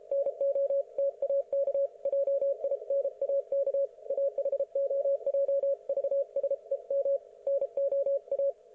Morse code, heard as a keyed tone, about 32 wpm, 560 Hz. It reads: NO TAK JINAK AHOJ VSEM NO A
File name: Archiv / CUC117 / ok1fhi cul117